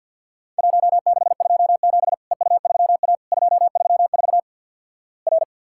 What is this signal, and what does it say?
Morse code, heard as a keyed tone, about 50 wpm, 700 Hz. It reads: 1627 EU3A 234 R
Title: Worked All Europe DX Contest - QTC Exchange - 50wpm